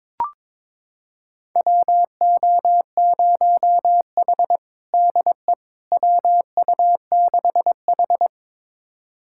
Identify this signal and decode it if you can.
Morse code, heard as a keyed tone, about 23 wpm, 700 Hz. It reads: WO0H DE WU6H
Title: USA Callsign DE Callsign - 4 Characters - 22wpm